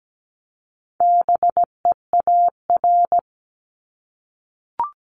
Morse code, heard as a keyed tone, about 17 wpm, 700 Hz. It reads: BEAR
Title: ING Suffix - Top 100 Words with Base Words 17wpm